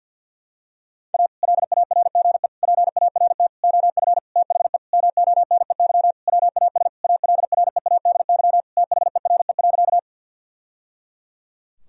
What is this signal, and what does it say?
Morse code, heard as a keyed tone, about 51 wpm, 700 Hz. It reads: A LARGE PART OF THE MONEY WAS ALREADY THERE.
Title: Sentences from Top 1000 Words - Review Part 1 - 50wpm